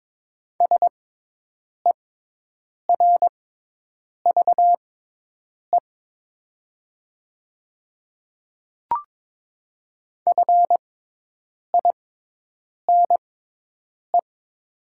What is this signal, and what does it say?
Morse code, heard as a keyed tone, about 22 wpm, spaced as for 8 wpm, 700 Hz. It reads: SERVE FINE
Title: 6x Character Spacing - Top 500 Words - 22wpm